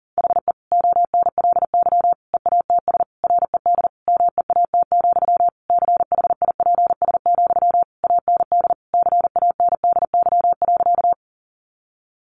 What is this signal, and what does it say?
Morse code, heard as a keyed tone, about 40 wpm, 700 Hz. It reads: HE ONLY EATS RED MEAT, CHIPS, AND CANDY.
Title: Easy Phrases 40wpm